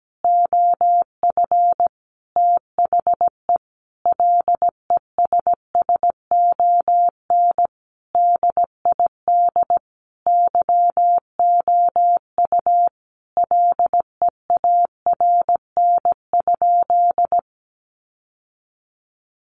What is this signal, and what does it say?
Morse code, heard as a keyed tone, about 17 wpm, 700 Hz. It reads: OF THE LESSON DID YOU LEARN?
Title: Sentences from Top 1200 Words 17wpm